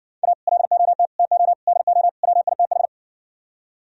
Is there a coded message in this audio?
Yes: Morse code from 0.2 to 2.9 s, heard as a keyed tone, about 58 wpm, 700 Hz. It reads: A LOT TO DO WITH